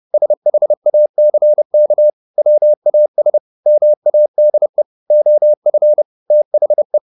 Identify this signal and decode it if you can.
Morse code, heard as a keyed tone, about 30 wpm, 600 Hz. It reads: SHACK WAS MADE OF THE
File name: the old man and the sea / the old man and the sea 1 30wpm